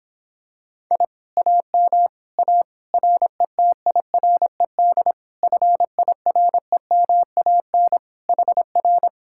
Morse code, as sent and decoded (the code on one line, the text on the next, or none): .. / .- -- / .- / .-. - .. .-. -.. / ..-. .. .-. -- .- -. / .... .-.
I AM A RETIRED FIREMAN HR